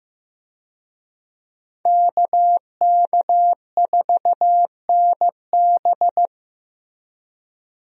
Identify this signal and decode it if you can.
Morse code, heard as a keyed tone, about 15 wpm, 700 Hz. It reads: KK4NB
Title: USA Call Signs - 5 Characters - Repeated 3x - 15wpm